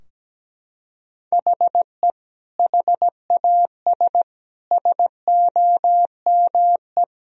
Morse code, heard as a keyed tone, about 17 wpm, 700 Hz. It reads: HE HAS SOME
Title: Sentences from Top 1000 Words - Review Part 3 - 17wpm